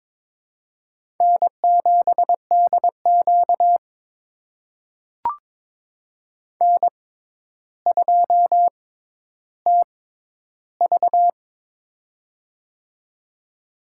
Morse code, heard as a keyed tone, about 22 wpm, spaced as for 11 wpm, 700 Hz. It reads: N7DQ N2TV
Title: USA Call Signs - 4 Characters - 6x Character Spacing - 22wpm